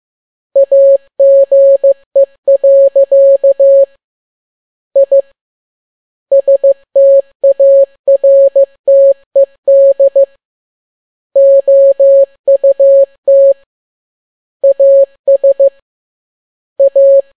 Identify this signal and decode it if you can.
Morse code, heard as a keyed tone, about 15 wpm, 550 Hz. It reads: AGE. I STARTED OUT AS A
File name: n6tt p1 15C13